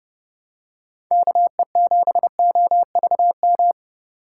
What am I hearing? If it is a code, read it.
Morse code, heard as a keyed tone, about 30 wpm, 700 Hz. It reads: KE7OVM
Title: USA Call Signs - 6 Characters - 4x Character Spacing - 30wpm